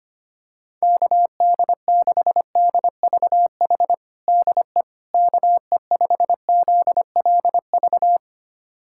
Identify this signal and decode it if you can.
Morse code, heard as a keyed tone, about 25 wpm, 700 Hz. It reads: KD6DVH DE KE5ZLV
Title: USA Callsign DE Callsign 25wpm